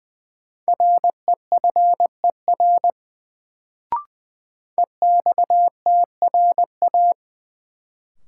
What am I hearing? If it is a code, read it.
Morse code, heard as a keyed tone, about 20 wpm, 700 Hz. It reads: REFER EXTRA
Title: Sentences from Top 1000 Words - New Word Drill - 20wpm